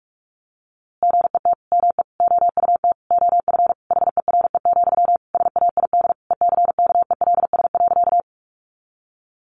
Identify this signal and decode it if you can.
Morse code, heard as a keyed tone, about 46 wpm, 700 Hz. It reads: GET ME OUT OF HERE, SAID ECKELS.
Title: A Sound Of Thunder 45wpm